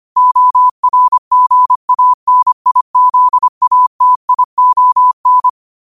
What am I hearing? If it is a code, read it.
Morse code, heard as a keyed tone, about 25 wpm, 1000 Hz. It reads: ORGANIZATION
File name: qotdfiles / 25wpm